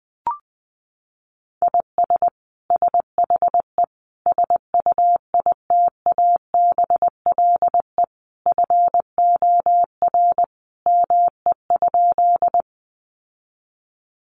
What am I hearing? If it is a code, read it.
Morse code, heard as a keyed tone, about 20 wpm, 700 Hz. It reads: IS SHE SUITABLE FOR ME?
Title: Sentences from Top 1400 Words 20wpm